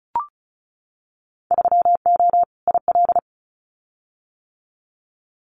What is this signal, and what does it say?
Morse code, heard as a keyed tone, about 36 wpm, 700 Hz. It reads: 3O IL
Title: Winter Field Day Exchanges 35wpm